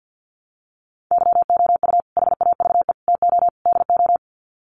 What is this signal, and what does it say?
Morse code, heard as a keyed tone, about 50 wpm, 700 Hz. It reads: YOU HAVE TO DO
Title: Sets of 4 Words 50wpm